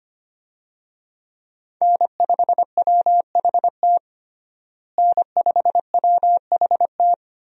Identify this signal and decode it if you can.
Morse code, heard as a keyed tone, about 25 wpm, 700 Hz. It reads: N5WHT N5WHT
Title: USA Call Signs - 5 Characters - Repeated 2x - 25wpm